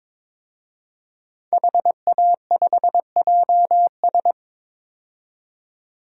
Morse code, heard as a keyed tone, about 22 wpm, 700 Hz. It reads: HA5JS
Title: DXCC-100 Call Signs - 5 Characters - 22wpm